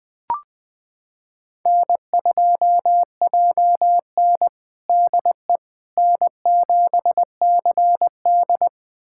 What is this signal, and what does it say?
Morse code, heard as a keyed tone, about 20 wpm, 700 Hz. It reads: N2JN DE N7CD
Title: USA Callsign DE Callsign - 4 Characters - 20wpm